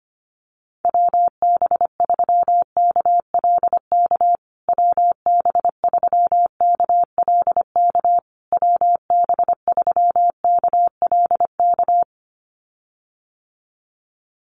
Morse code, heard as a keyed tone, about 25 wpm, 700 Hz. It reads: WB3KLK WB3KLK WB3KLK
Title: USA Call Signs Repeated 3x 25wpm